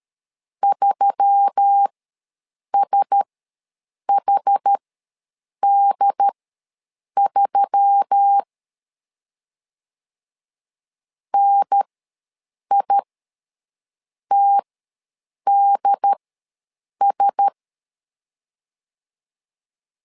Morse code, heard as a keyed tone, about 13 wpm, spaced as for 7 wpm, 800 Hz. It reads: 3SHD3 NITDS
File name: LEZIONI / 4 Q1S3 ws 12wpm 2038